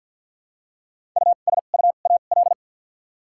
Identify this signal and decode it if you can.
Morse code, heard as a keyed tone, about 53 wpm, spaced as for 37 wpm, 700 Hz. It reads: USUAL
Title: 2x Character Spacing - Top 500 Words - 50wpm